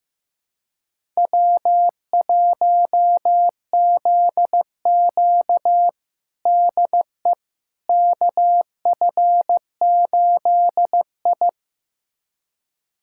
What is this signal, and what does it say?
Morse code, heard as a keyed tone, about 15 wpm, 700 Hz. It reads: W1ZQ DE KF8I
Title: USA Callsign DE Callsign - 4 Characters - 15wpm